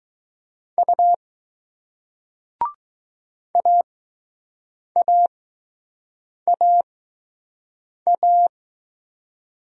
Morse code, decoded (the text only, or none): U AAAA